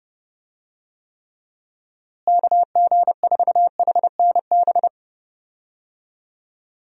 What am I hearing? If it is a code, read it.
Morse code, heard as a keyed tone, about 29 wpm, 700 Hz. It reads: KG4HNB
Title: USA Call Signs - 6 Characters - Repeated 3x - 30wpm